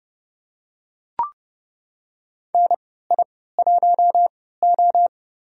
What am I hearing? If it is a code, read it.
Morse code, heard as a keyed tone, about 30 wpm, spaced as for 17 wpm, 700 Hz. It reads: NI1O
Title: USA Call Signs - 4 Characters - 3x Character Spacing - 30wpm